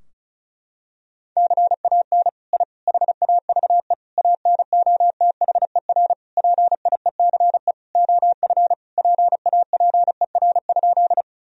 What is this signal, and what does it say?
Morse code, heard as a keyed tone, about 35 wpm, 700 Hz. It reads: CAN I HAVE ANOTHER PIECE OF PAPER?
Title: Sentences from Top 1000 Words - Review Part 2 - 35wpm